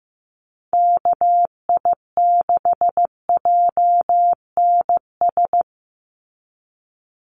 Morse code, heard as a keyed tone, about 15 wpm, 700 Hz. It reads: KI6JNS
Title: USA Call Signs 15wpm